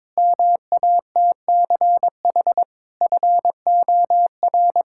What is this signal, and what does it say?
Morse code, heard as a keyed tone, about 22 wpm, 700 Hz. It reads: MATCH FOR
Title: Sentences from Top 1600 Words 22wpm